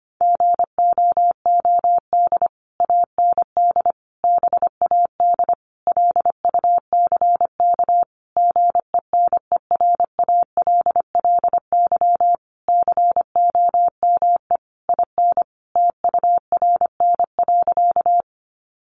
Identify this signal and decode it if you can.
Morse code, heard as a keyed tone, about 25 wpm, 700 Hz. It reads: GOOD AND BAD LUCK GENERALLY COME IN TURN.
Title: Sentences from Top 1500 Words 25wpm